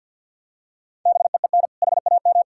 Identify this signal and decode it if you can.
Morse code, heard as a keyed tone, about 52 wpm, 700 Hz. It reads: BEEN HAM
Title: Practice QSO - Exchange 3 - More Info - 50wpm